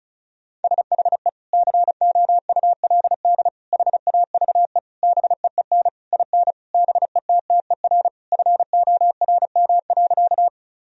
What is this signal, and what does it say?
Morse code, heard as a keyed tone, about 34 wpm, 700 Hz. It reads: SHE COULD HAVE BEEN IN BETTER FORM.
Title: Sentences from Top 1000 Words - Review Part 1 - 35wpm